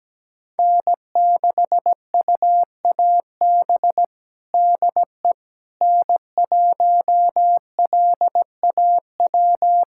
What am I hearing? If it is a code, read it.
Morse code, heard as a keyed tone, about 17 wpm, 700 Hz. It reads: N6UAB DE N1LAW
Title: USA Callsign DE Callsign - 5 Characters - 17wpm